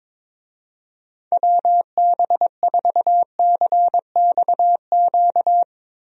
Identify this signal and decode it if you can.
Morse code, heard as a keyed tone, about 22 wpm, 700 Hz. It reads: WB4CXQ
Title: USA Call Signs - 6 Characters - Repeated 2x - 22wpm